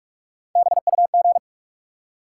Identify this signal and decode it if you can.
Morse code, heard as a keyed tone, about 45 wpm, 700 Hz. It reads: BUG
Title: Three Letter Words - 45wpm